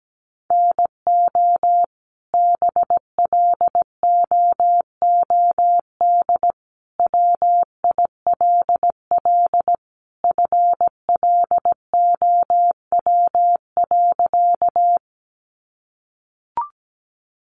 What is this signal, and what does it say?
Morse code, heard as a keyed tone, about 17 wpm, 700 Hz. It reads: NO BLOOD WILL FLOW.
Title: Sentences from Top 1200 Words 17wpm